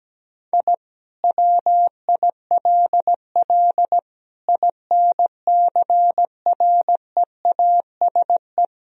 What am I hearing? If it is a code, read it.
Morse code, heard as a keyed tone, about 17 wpm, 700 Hz. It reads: I WILL INCREASE
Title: Sentences from Top 1000 Words - Review Part 4 - 17wpm